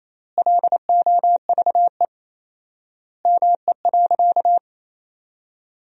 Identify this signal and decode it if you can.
Morse code, heard as a keyed tone, about 28 wpm, 700 Hz. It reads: LOVE ME.
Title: 4x Word-Spacing - Sentences from Top 400 Words - 28wpm